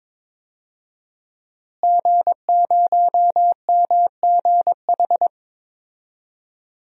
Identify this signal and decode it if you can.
Morse code, heard as a keyed tone, about 22 wpm, 700 Hz. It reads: G0MGH
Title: DXCC-100 Call Signs - 5 Characters - Repeated 2x - 22wpm